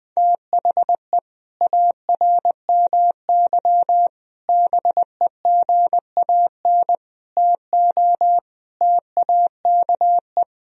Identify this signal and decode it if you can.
Morse code, heard as a keyed tone, about 20 wpm, 700 Hz. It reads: THE ARMY BEGAN TO TAKE
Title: Sentences from Top 1000 Words - Review Part 3 - 20wpm